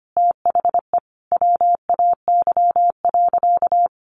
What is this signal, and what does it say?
Morse code, heard as a keyed tone, about 25 wpm, 700 Hz. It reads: THE WAY.
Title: Sentences from Top 400 Words 25wpm